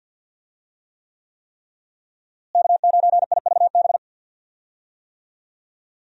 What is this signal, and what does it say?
Morse code, heard as a keyed tone, about 51 wpm, 700 Hz. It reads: K9IVB
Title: USA Call Signs - 5 Characters - Repeated 3x - 50wpm